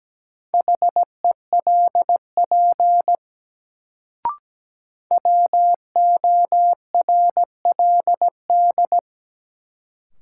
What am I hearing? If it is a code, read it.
Morse code, heard as a keyed tone, about 17 wpm, 700 Hz. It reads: HELP WORLD
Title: Sentences from Top 200 Words - New Word Drill - 17wpm